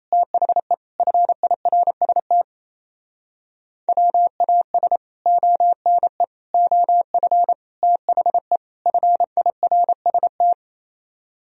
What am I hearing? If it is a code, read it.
Morse code, heard as a keyed tone, about 37 wpm, 700 Hz. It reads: THE FIRST WAS ONE OF THE FIRST